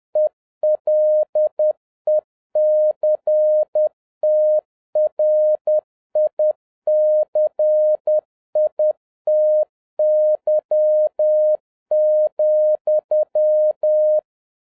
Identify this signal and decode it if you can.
Morse code, heard as a keyed tone, about 10 wpm, 610 Hz. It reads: ELECTRICITY,
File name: 2011-07-mp3 / n36 THE 3.11 2 10wpm